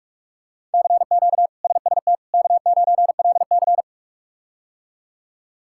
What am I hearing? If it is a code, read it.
Morse code, heard as a keyed tone, about 43 wpm, 700 Hz. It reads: CQ SST K9LC
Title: K1USN SST QSO Practice - 45wpm